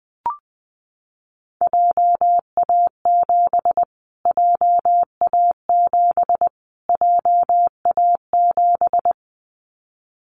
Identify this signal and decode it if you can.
Morse code, heard as a keyed tone, about 20 wpm, 700 Hz. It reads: JA7 JA7 JA7
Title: Callsign Prefixes for Top 25 DXCC - Repeated 3x - 20wpm